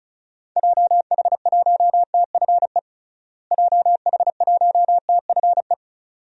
Morse code, heard as a keyed tone, about 34 wpm, 700 Hz. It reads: JH1TFE JH1TFE
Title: DXCC-100 Call Signs - 6 Characters - Repeated 2x - 35wpm